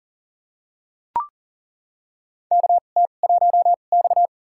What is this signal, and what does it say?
Morse code, heard as a keyed tone, about 40 wpm, spaced as for 29 wpm, 700 Hz. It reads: KT1X
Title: USA Call Signs - 2x Character Spacing - 40wpm